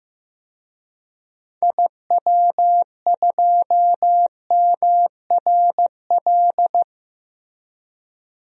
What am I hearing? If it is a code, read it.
Morse code, heard as a keyed tone, about 15 wpm, 700 Hz. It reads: IW2MRL
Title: DXCC-100 Call Signs - 6 Characters - Repeated 3x - 15wpm